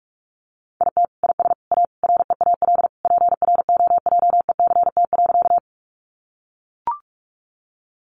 Morse code, heard as a keyed tone, about 44 wpm, 700 Hz. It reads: IT IS A REAL PROJECT.
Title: Sentences from Top 1400 Words 45wpm